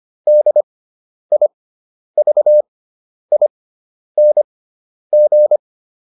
Morse code, heard as a keyed tone, about 25 wpm, spaced as for 10 wpm, 600 Hz. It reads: DIVING